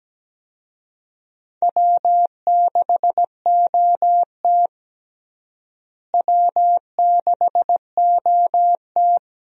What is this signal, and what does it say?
Morse code, heard as a keyed tone, about 17 wpm, 700 Hz. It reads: W6OT W6OT
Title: USA Call Signs - 4 Characters - Repeated 3x - 17wpm